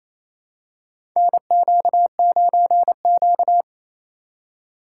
Morse code, heard as a keyed tone, about 28 wpm, 700 Hz. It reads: NQ9Q
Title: USA Call Signs - 4 Characters - 2x Character Spacing - 28wpm